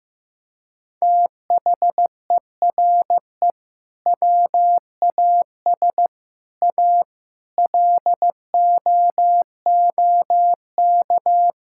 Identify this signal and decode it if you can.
Morse code, heard as a keyed tone, about 15 wpm, 700 Hz. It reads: THERE WAS A LOOK
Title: Sentences from Top 1600 Words 15wpm